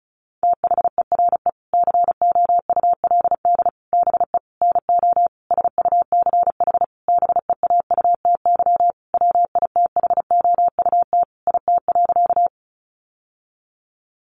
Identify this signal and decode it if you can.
Morse code, heard as a keyed tone, about 34 wpm, 700 Hz. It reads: THERE COULD BE NO SUCH BEAUTY WITHOUT IT.
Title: Sentences from Top 600 Words 35wpm